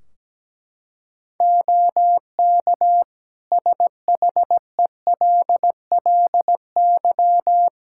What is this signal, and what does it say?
Morse code, heard as a keyed tone, about 17 wpm, 700 Hz. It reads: OK SHELLY
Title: Practice QSO - Exchange 3 - More Info - 17wpm